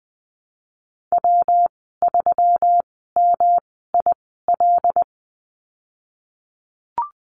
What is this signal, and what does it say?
Morse code, heard as a keyed tone, about 20 wpm, spaced as for 15 wpm, 700 Hz. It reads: W3MIL
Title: 122 Lesson 20g - Number 3 - Call Signs 20wpm